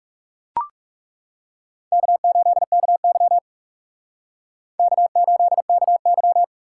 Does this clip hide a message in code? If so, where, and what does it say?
Yes: Morse code from 1.9 to 6.4 s, heard as a keyed tone, about 38 wpm, 700 Hz. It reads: K8KY K8KY